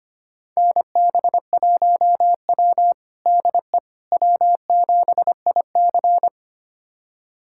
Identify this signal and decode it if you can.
Morse code, heard as a keyed tone, about 25 wpm, 700 Hz. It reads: NB1W DE W7IC
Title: USA Callsign DE Callsign - 4 Characters - 25wpm